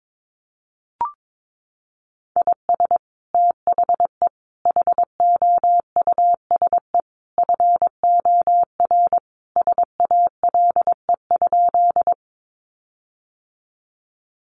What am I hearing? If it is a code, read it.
Morse code, heard as a keyed tone, about 22 wpm, 700 Hz. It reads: IS THE HOUSE FOR SALE?
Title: Sentences from Top 1400 Words 22wpm